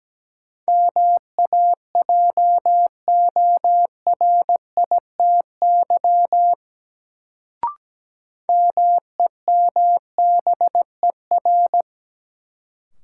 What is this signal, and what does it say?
Morse code, heard as a keyed tone, about 17 wpm, 700 Hz. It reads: MAJORITY MEMBER
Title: Sets of 3 Words - New Word Drill - 17wpm